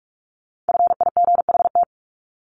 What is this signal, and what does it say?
Morse code, heard as a keyed tone, about 44 wpm, 700 Hz. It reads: FIGHT
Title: ING Suffix - Top 100 Words with Base Words 45wpm